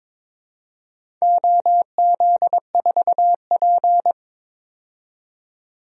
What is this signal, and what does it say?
Morse code, heard as a keyed tone, about 22 wpm, 700 Hz. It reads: OZ4P
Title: DXCC-100 Call Signs - 4 Characters - 22wpm